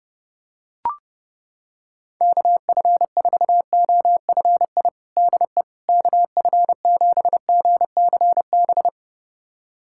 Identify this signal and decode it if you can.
Morse code, heard as a keyed tone, about 30 wpm, 700 Hz. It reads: KF4OFI DE KF7GCB
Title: USA Callsign DE Callsign - 6 Characters - 30wpm